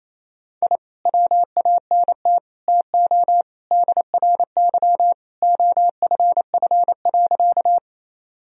Morse code, heard as a keyed tone, about 28 wpm, 700 Hz. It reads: I WANT TO DRY OFF.
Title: Sentences from Top 1000 Words - Review Part 3 - 28wpm